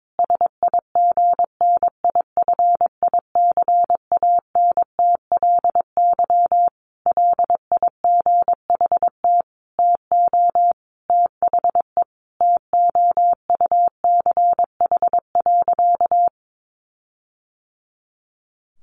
Morse code, heard as a keyed tone, about 22 wpm, 700 Hz. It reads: SIGNIFICANTLY LIGHT TO THE TOUCH.
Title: Sentences from Top 1500 Words 22wpm